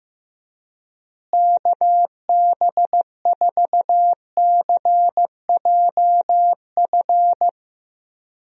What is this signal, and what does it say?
Morse code, heard as a keyed tone, about 15 wpm, 700 Hz. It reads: KB4CJF
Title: USA Call Signs - 6 Characters - Repeated 2x - 15wpm